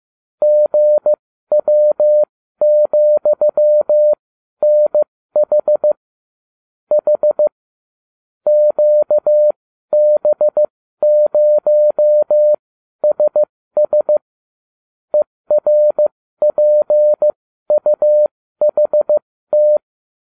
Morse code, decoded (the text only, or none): GW,NH H QB0SS ERPUHT